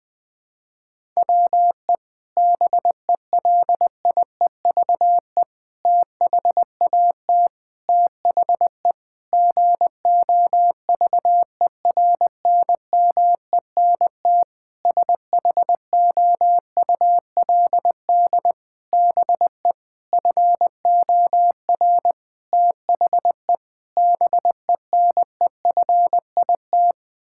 Morse code, heard as a keyed tone, about 20 wpm, 700 Hz. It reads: WE BELIEVE THAT THE GOVERNMENT SHOULD BE FOR THE BENEFIT
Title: Sentences from Top 1000 Words - Review Part 6 - 20wpm